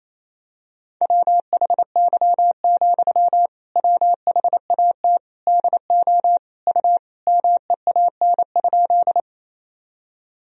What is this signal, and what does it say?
Morse code, heard as a keyed tone, about 28 wpm, 700 Hz. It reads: WHY, WHAT DO U MEAN?
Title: Sentences from Top 200 Words and Abbreviations 28wpm